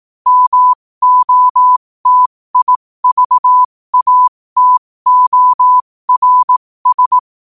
Morse code, heard as a keyed tone, about 18 wpm, spaced as for 15 wpm, 1000 Hz. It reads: MOTIVATORS
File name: qotdfiles / 15wpm